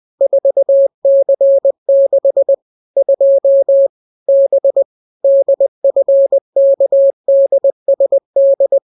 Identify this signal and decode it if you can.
Morse code, heard as a keyed tone, about 20 wpm, 550 Hz. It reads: 4C6 2 B DFKDSD